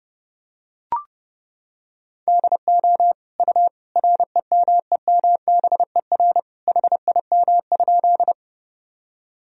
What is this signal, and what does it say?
Morse code, heard as a keyed tone, about 30 wpm, 700 Hz. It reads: DO U REMEMBER HIM?
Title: Sentences from Top 400 Words and Abbreviations 30wpm